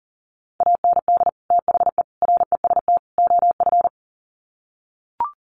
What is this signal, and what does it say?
Morse code, heard as a keyed tone, about 40 wpm, 700 Hz. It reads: AND THE REST OF
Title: Sets of 4 Words 40wpm